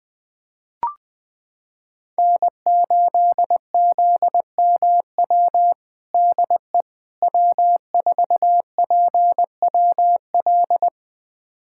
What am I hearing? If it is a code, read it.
Morse code, heard as a keyed tone, about 21 wpm, 700 Hz. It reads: N8ZMW DE W4PWL
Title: USA Callsign DE Callsign - 5 Characters - 20wpm